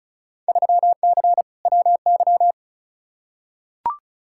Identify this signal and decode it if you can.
Morse code, heard as a keyed tone, about 36 wpm, 700 Hz. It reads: 3C WY
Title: Field Day Exchanges Encore - 35wpm